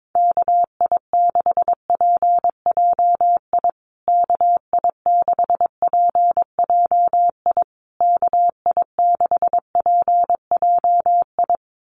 Morse code, heard as a keyed tone, about 22 wpm, 700 Hz. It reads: KI6PJI KI6PJI KI6PJI
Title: USA Call Signs Repeated 3x 22wpm